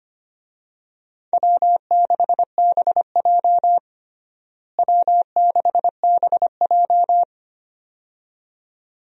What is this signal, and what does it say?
Morse code, heard as a keyed tone, about 25 wpm, 700 Hz. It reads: W6BJ W6BJ
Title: USA Call Signs - 4 Characters - Repeated 2x - 25wpm